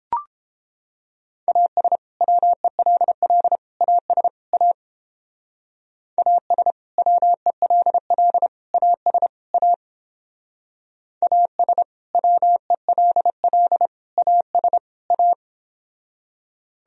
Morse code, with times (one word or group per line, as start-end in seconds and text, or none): 1.5-2.0 AS
2.2-3.6 WELL
3.8-4.3 AS
4.5-4.7 A
6.2-6.7 AS
7.0-8.5 WELL
8.7-9.3 AS
9.5-9.7 A
11.2-11.8 AS
12.1-13.9 WELL
14.2-14.8 AS
15.1-15.3 A